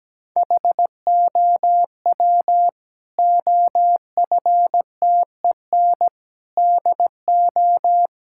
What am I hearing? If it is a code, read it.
Morse code, heard as a keyed tone, about 17 wpm, 700 Hz. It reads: HOW OFTEN DO
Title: Sentences from Top 1000 Words - Review Part 6 - 17wpm